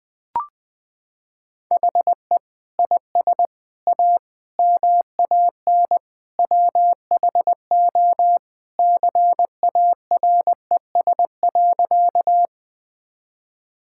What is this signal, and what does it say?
Morse code, heard as a keyed tone, about 20 wpm, 700 Hz. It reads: HE IS A MAN WHO CARES.
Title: Sentences with 3-Word Phrases - 20wpm